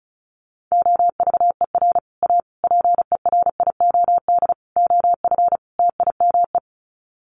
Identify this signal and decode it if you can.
Morse code, heard as a keyed tone, about 34 wpm, 700 Hz. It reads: OVER A PERIOD OF TIME
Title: Sets of 5 Words Encore2 35wpm